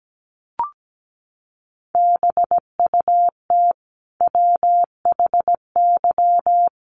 Morse code, heard as a keyed tone, about 17 wpm, 700 Hz. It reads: BUT WHY
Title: Sentences from Top 1400 Words 17wpm